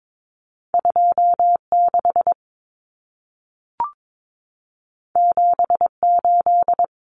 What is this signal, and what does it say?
Morse code, heard as a keyed tone, about 22 wpm, 700 Hz. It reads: 26 78
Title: Numbers - 2-Digit 22wpm